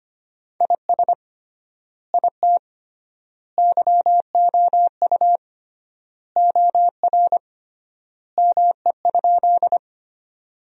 Morse code, heard as a keyed tone, about 25 wpm, 700 Hz. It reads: IS IT YOU OR ME?
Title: 3x Word-Spacing - Sentences from Top 200 Words - 25wpm